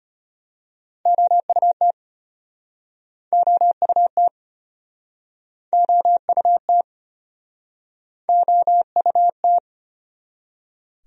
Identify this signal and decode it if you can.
Morse code, heard as a keyed tone, about 36 wpm, 700 Hz. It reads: OUT OUT OUT OUT